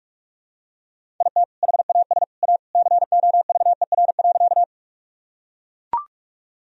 Morse code, heard as a keyed tone, about 44 wpm, 700 Hz. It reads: IT HAS A COVER.
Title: Sentences from Top 1000 Words - Review Part 1 - 45wpm